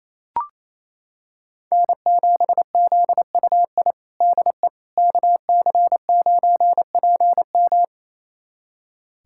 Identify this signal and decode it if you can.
Morse code, heard as a keyed tone, about 28 wpm, 700 Hz. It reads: N7ZUI DE KC9PM
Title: USA Callsign DE Callsign - 5 Characters - 28wpm